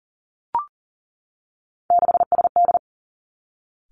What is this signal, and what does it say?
Morse code, heard as a keyed tone, about 40 wpm, 690 Hz. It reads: 6SD
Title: Three Challenging Letter-Number Combinations 40wpm